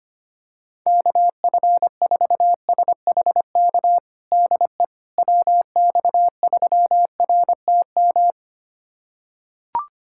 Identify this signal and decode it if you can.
Morse code, heard as a keyed tone, about 25 wpm, 700 Hz. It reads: KF4SHK DE WX3RTM
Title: USA Callsign DE Callsign - 6 Characters - 25wpm